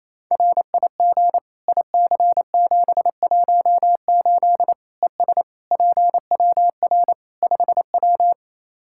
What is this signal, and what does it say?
Morse code, heard as a keyed tone, about 28 wpm, 700 Hz. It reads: RIG IC718 ES PWR 5W
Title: K1USN SST QSO Practice 28wpm